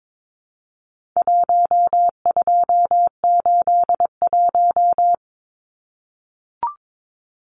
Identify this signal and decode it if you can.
Morse code, heard as a keyed tone, about 22 wpm, 700 Hz. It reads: 1281
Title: Numbers - 4-Digit 22wpm